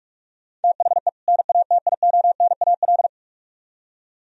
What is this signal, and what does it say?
Morse code, heard as a keyed tone, about 44 wpm, 700 Hz. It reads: THE NATIONAL